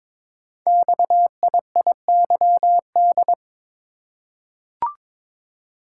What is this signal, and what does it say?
Morse code, heard as a keyed tone, about 22 wpm, 700 Hz. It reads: XIIYD
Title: Code Groups - 5 Letter:Numbers - 22wpm